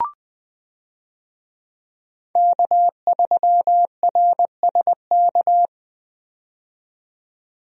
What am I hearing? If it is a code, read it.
Morse code, heard as a keyed tone, about 21 wpm, 700 Hz. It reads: K3RSK
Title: USA Call Signs - 5 Characters - 20wpm